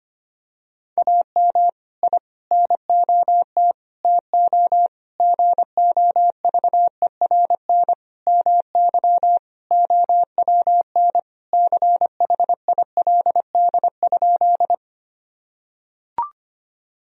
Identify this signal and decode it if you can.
Morse code, heard as a keyed tone, about 25 wpm, 700 Hz. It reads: AM I NOT TO GOVERN MY OWN CHILD?
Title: Sentences from Top 1000 Words - Review Part 2 - 25wpm